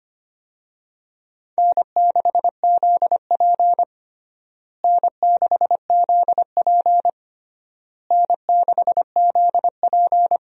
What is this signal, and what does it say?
Morse code, heard as a keyed tone, about 25 wpm, 700 Hz. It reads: N6ZP N6ZP N6ZP
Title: USA Call Signs - 4 Characters - Repeated 3x - 25wpm